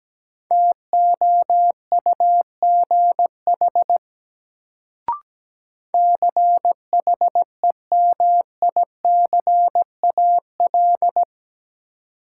Morse code, heard as a keyed tone, about 17 wpm, 700 Hz. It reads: TOUGH CHEMICAL
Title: Sentences from Top 1200 Words - New Word Drill Alt - 17wpm